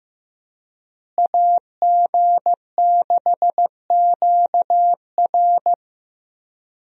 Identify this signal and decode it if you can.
Morse code, heard as a keyed tone, about 15 wpm, 700 Hz. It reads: AG6QR
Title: USA Call Signs - 5 Characters - Repeated 3x - 15wpm